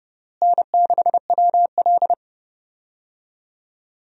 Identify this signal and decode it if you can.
Morse code, heard as a keyed tone, about 30 wpm, 700 Hz. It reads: N6WL
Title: USA Call Signs - 4 Characters - Repeated 3x - 30wpm